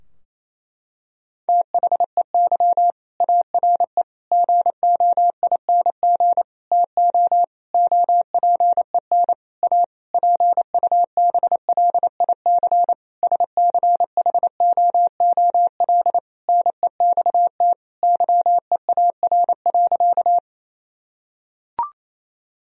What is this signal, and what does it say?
Morse code, heard as a keyed tone, about 28 wpm, 700 Hz. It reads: THEY ARE GOING TO OPEN A PUBLIC SCHOOL NEXT YEAR.
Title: Sentences from Top 1000 Words - Review Part 3 - 28wpm